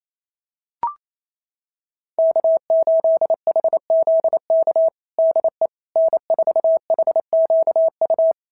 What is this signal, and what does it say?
Morse code, heard as a keyed tone, about 27 wpm, 650 Hz. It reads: K8HZK DE N4HQU
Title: SOTA QSO Practice 28wpm